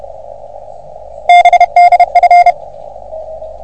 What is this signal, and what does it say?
Morse code, heard as a keyed tone, about 30 wpm, 720 Hz. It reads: BDF
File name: OM3PA PILUP CUC 388